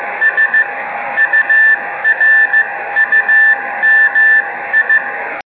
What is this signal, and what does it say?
Morse code, heard as a keyed tone, about 15 wpm, spaced as for 12 wpm, 1750 Hz, with noise. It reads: SURUMI
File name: Beacon / 5GbeaconJH1UGF awasaki